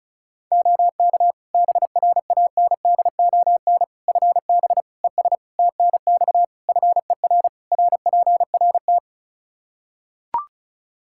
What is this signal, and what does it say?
Morse code, heard as a keyed tone, about 34 wpm, 700 Hz. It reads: OK BRANDON FB ES TNX FER RPRT
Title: K1USN SST QSO Practice 35wpm